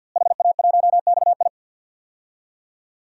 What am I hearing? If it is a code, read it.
Morse code, heard as a keyed tone, about 52 wpm, 700 Hz. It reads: HA1XI
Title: DXCC-100 Call Signs - 5 Characters - Repeated 3x - 50wpm